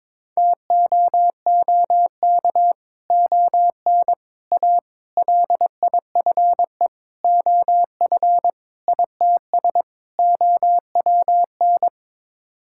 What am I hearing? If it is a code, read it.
Morse code, heard as a keyed tone, about 22 wpm, 700 Hz. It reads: TOOK ON A LIFE OF ITS OWN
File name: practice / 22wpm / Sets of 7 Words - 22wpm